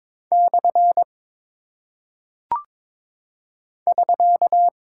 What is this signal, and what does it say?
Morse code, heard as a keyed tone, about 22 wpm, 700 Hz. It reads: / <SK>
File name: practice / 22wpm / Single Punctuation-Prosign - 22wpm